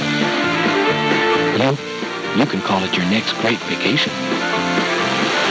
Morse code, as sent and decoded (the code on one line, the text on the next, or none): none